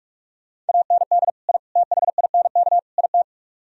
Morse code, heard as a keyed tone, about 48 wpm, 700 Hz. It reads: AND I THINK IT